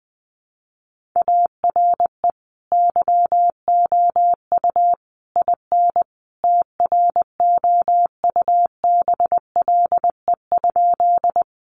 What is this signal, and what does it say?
Morse code, heard as a keyed tone, about 20 wpm, 700 Hz. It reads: ARE YOU IN TROUBLE?
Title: Sentences from Top 1200 Words 20wpm